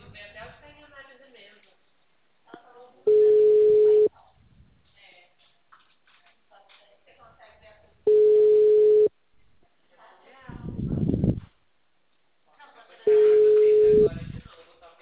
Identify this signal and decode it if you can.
Ringback tone (Europe, 425 Hz), with speech.
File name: out- -2016- - - .10631